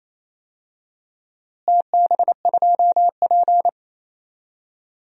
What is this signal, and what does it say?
Morse code, heard as a keyed tone, about 28 wpm, 700 Hz. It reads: TB2P
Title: DXCC-100 Call Signs - 4 Characters - Repeated 2x - 28wpm